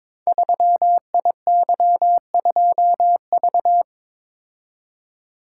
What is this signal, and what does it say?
Morse code, heard as a keyed tone, about 22 wpm, 700 Hz. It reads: 3IY2V
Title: Code Groups - 5 Letter:Numbers - 22wpm